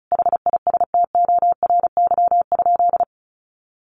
Morse code, heard as a keyed tone, about 34 wpm, 700 Hz. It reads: HISTORY?
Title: Sentences from Top 1500 Words 35wpm